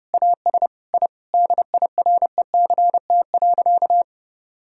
Morse code, heard as a keyed tone, about 32 wpm, 700 Hz. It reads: AS I DIRECT.
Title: Sentences from Top 1000 Words - Review Part 2 - 30wpm